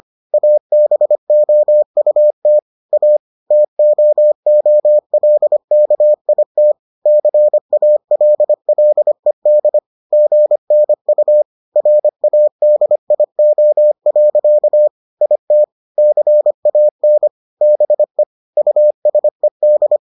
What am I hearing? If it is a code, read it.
Morse code, heard as a keyed tone, about 25 wpm, 600 Hz. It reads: ABOUT A TOOLKIT CALLED GNU RADIO. IT CAN BE USED